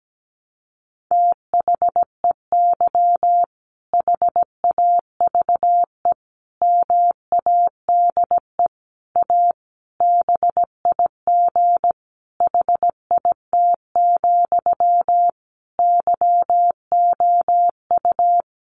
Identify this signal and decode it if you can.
Morse code, heard as a keyed tone, about 17 wpm, 700 Hz. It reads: THEY HAVE MADE A BIG HIT, YOU
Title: Sentences from Top 1200 Words 17wpm